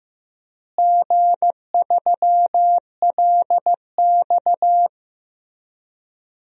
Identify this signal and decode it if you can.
Morse code, heard as a keyed tone, about 16 wpm, 700 Hz. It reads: G3LX
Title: DXCC-100 Call Signs - 4 Characters - Repeated 2x - 15wpm